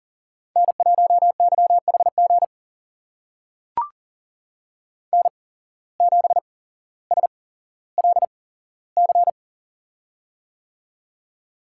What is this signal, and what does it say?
Morse code, heard as a keyed tone, about 40 wpm, spaced as for 17 wpm, 700 Hz. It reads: N1YHG N7SLC
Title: USA Call Signs - 5 Characters - 8x Character Spacing - 40wpm